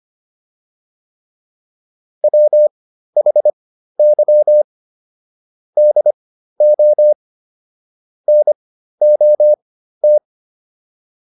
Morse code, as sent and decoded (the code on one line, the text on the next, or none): .-- .... -.-- / -.. --- / -. --- -
WHY DO NOT